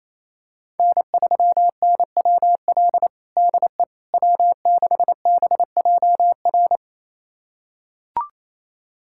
Morse code, heard as a keyed tone, about 27 wpm, 700 Hz. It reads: N3NWL DE W6BJR
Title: USA Callsign DE Callsign - 5 Characters - 28wpm